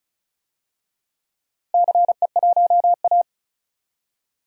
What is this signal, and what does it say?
Morse code, heard as a keyed tone, about 36 wpm, 700 Hz. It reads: CE1A
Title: DXCC-100 Call Signs - 4 Characters - Repeated 2x - 35wpm